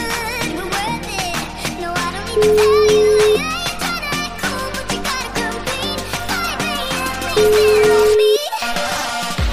Ringback tone (Europe, 425 Hz), with noise.